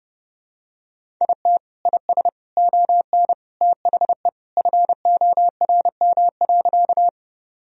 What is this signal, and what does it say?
Morse code, heard as a keyed tone, about 30 wpm, 700 Hz. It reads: IT IS ON THE FORM.
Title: Sentences from Top 1000 Words - Review Part 1 - 30wpm